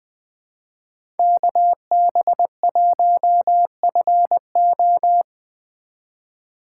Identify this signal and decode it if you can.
Morse code, heard as a keyed tone, about 25 wpm, 700 Hz. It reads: KB1FO
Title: Speed Racing - USA 2x2 Call Signs 20wpm